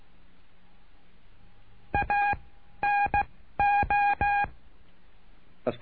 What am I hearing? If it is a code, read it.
Morse code, heard as a keyed tone, about 16 wpm, 810 Hz. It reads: A N O
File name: cursos / CURSO DE TELEGRAFIA / Telegrafia - Trilha 09